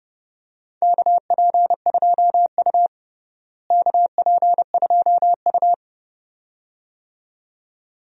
Morse code, heard as a keyed tone, about 30 wpm, 700 Hz. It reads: KP2U KP2U
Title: DXCC-100 Call Signs - 4 Characters - Repeated 2x - 30wpm